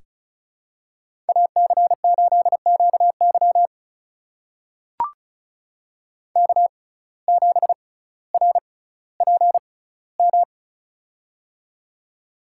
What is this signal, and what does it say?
Morse code, heard as a keyed tone, about 35 wpm, spaced as for 18 wpm, 700 Hz. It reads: AC8QY K7RPM
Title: USA Call Signs - 5 Characters - 6x Character Spacing - 35wpm